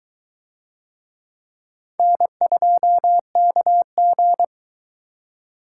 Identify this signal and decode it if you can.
Morse code, heard as a keyed tone, about 21 wpm, 700 Hz. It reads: N2KG